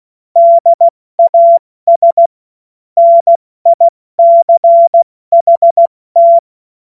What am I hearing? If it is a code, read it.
Morse code, heard as a keyed tone, about 16 wpm, 680 Hz. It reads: DAS NICHT
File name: Morsekurs / Witze / Witz 13 Februar 2024 70 F80